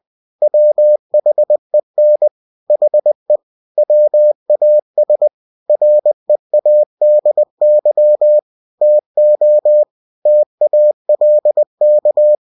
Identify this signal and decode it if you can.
Morse code, heard as a keyed tone, about 20 wpm, 600 Hz. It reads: WHEN HE WAS READY TO TALK
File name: cw-books / winnie-the-pooh / winnie-20wpm-page-0098